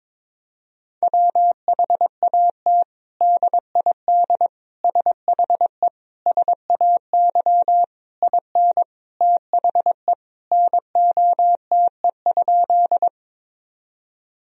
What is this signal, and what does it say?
Morse code, heard as a keyed tone, about 22 wpm, 700 Hz. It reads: WHAT DID SHE SAY IN THE NOTE?
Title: Sentences from Top 1000 Words - Review Part 2 - 22wpm